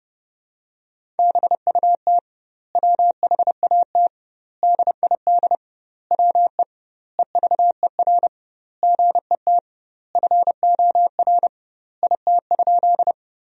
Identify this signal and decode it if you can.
Morse code, heard as a keyed tone, about 30 wpm, 700 Hz. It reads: BUT WHAT DID WE EVER GET FOR IT?
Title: 2x Word-Spacing - Sentences from Top 400 Words - 30wpm